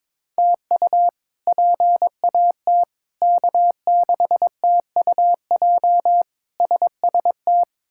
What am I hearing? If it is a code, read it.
Morse code, heard as a keyed tone, about 22 wpm, 700 Hz. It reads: TU PAT K6TUJ SST
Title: K1USN SST QSO Practice - 22wpm